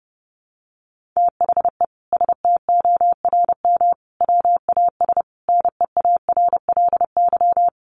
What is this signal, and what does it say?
Morse code, heard as a keyed tone, about 30 wpm, 700 Hz. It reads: THE STORM WAS NEARLY
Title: Sentences from Top 1500 Words 30wpm